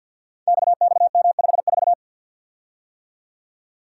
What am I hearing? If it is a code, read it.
Morse code, heard as a keyed tone, about 51 wpm, 700 Hz. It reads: XXM54
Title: Code Groups - 5 Letter:Numbers - 50wpm